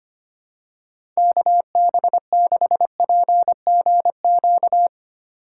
Morse code, heard as a keyed tone, about 25 wpm, 700 Hz. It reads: KB6PGQ
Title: USA Call Signs - 6 Characters - 4x Character Spacing - 25wpm